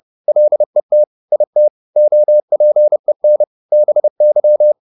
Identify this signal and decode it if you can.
Morse code, heard as a keyed tone, about 30 wpm, 600 Hz. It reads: LET IT OPEN BY
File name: the old man and the sea / the old man and the sea 1 30wpm